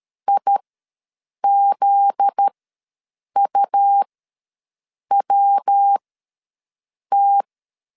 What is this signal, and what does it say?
Morse code, heard as a keyed tone, about 13 wpm, spaced as for 6 wpm, 800 Hz. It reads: IZUWT
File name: LEZIONI / 7 GF9Z / 7 GF9Z ws 12wpm 3321